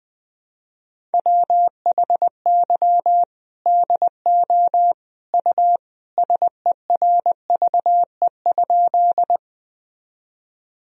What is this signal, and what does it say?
Morse code, heard as a keyed tone, about 20 wpm, 700 Hz. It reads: WHY DO U SERVE?
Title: Sentences from Top 400 Words and Abbreviations 20wpm